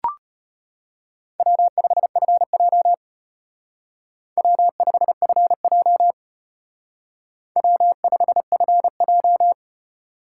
Morse code, heard as a keyed tone, about 33 wpm, 700 Hz. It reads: W5FJ W5FJ W5FJ